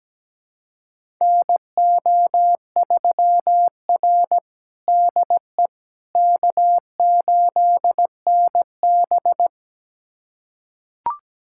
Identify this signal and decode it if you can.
Morse code, heard as a keyed tone, about 17 wpm, 700 Hz. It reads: NO3R DE K8NB
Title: USA Callsign DE Callsign - 4 Characters - 17wpm